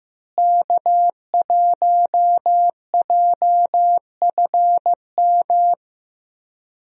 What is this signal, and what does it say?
Morse code, heard as a keyed tone, about 15 wpm, 700 Hz. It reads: K1JFM
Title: USA Call Signs - 5 Characters - Repeated 2x - 15wpm